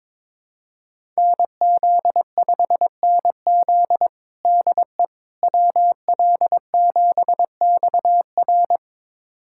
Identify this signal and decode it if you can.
Morse code, heard as a keyed tone, about 22 wpm, 700 Hz. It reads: NZ5NZ DE WL7XR
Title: USA Callsign DE Callsign - 5 Characters - 22wpm